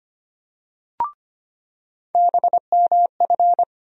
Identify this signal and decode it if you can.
Morse code, heard as a keyed tone, about 25 wpm, 700 Hz. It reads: BMF
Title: USA Call Sign Postfixes - 3 Letters 25wpm